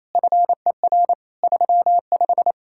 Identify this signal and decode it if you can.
Morse code, heard as a keyed tone, about 36 wpm, 700 Hz. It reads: FER 35